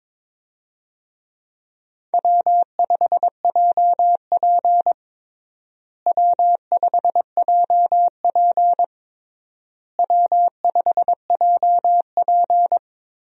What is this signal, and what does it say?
Morse code, heard as a keyed tone, about 22 wpm, 700 Hz. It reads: W5JP W5JP W5JP
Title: USA Call Signs - 4 Characters - Repeated 3x - 22wpm